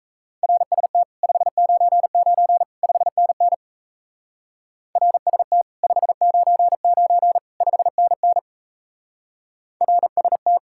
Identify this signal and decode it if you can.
Morse code, heard as a keyed tone, about 34 wpm, 700 Hz. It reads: RST 599 5NN RST 599 5NN RST
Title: Speed-Racing Common QSO Phrases - 28wpm